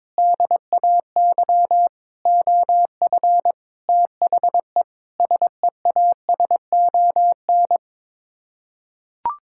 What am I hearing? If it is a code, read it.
Morse code, heard as a keyed tone, about 22 wpm, 700 Hz. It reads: DAY OF THE SEASON
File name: practice / 22wpm / Sets of 7 Words - 22wpm